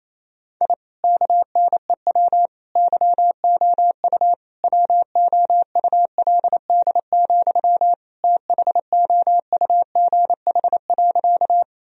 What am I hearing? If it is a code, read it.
Morse code, heard as a keyed tone, about 28 wpm, 700 Hz. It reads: I KNEW YOU WOULD, THOUGH.
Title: Sentences from Top 1000 Words - Review Part 2 - 28wpm